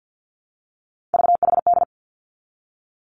Morse code, heard as a keyed tone, about 50 wpm, 700 Hz. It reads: VHD
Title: Three Challenging Letter-Number Combinations 50wpm